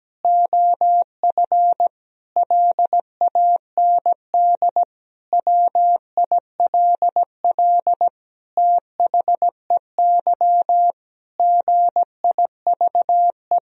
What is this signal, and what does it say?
Morse code, heard as a keyed tone, about 17 wpm, 700 Hz. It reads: OF LAND WILL THEY GIVE
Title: Sentences from Top 1000 Words - Review Part 2 - 17wpm